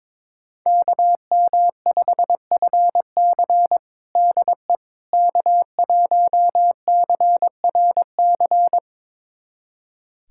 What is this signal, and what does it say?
Morse code, heard as a keyed tone, about 22 wpm, 700 Hz. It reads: KM5FC DE K1CRC
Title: USA Callsign DE Callsign - 5 Characters - 22wpm